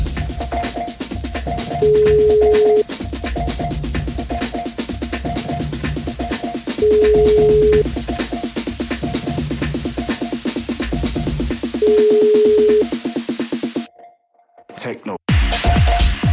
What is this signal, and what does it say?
Ringback tone (Europe, 425 Hz), with speech.